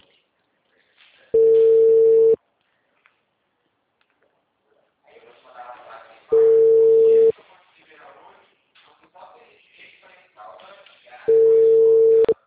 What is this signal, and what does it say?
Ringback tone (China, 450 Hz), with speech.
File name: out- -2020- - - .57777